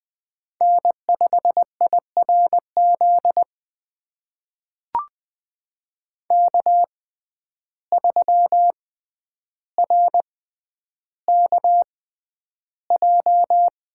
Morse code, heard as a keyed tone, about 20 wpm, spaced as for 10 wpm, 700 Hz. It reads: N5IRZ K3RKJ
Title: USA Call Signs - 5 Characters - 6x Character Spacing - 20wpm